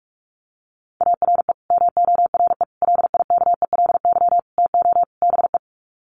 Morse code, heard as a keyed tone, about 44 wpm, 700 Hz. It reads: ARE MORE LIKELY TO BE
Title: Sets of 5 Words 45wpm